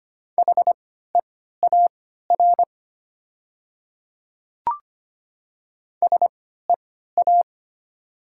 Morse code, heard as a keyed tone, about 25 wpm, spaced as for 14 wpm, 700 Hz. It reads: HEAR SEA
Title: 3x Character Spacing - Top 300 Words - 25wpm